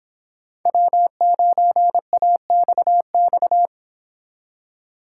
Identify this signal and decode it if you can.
Morse code, heard as a keyed tone, about 25 wpm, 700 Hz. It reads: W9AXX
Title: Speed Racing - USA Call Signs 22wpm